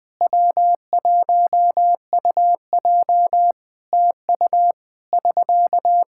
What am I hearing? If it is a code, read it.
Morse code, heard as a keyed tone, about 20 wpm, 700 Hz. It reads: W1UJ TU <SK>
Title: Practice QSO - Exchange 4 - Ending - 20wpm